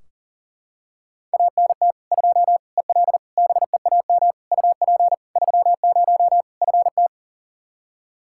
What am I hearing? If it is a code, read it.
Morse code, heard as a keyed tone, about 40 wpm, 700 Hz. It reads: ANT 2 EL BEAM UP 30 FT
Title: Practice QSO - Exchange 2 - Info - 40wpm